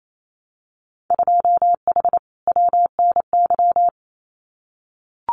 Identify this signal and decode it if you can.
Morse code, heard as a keyed tone, about 28 wpm, 700 Hz. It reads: 2H WNY
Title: Winter Field Day Exchanges Encore 28wpm